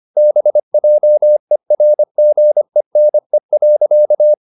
Morse code, heard as a keyed tone, about 25 wpm, 600 Hz. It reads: BJERGENE.
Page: Ferie 25 wpm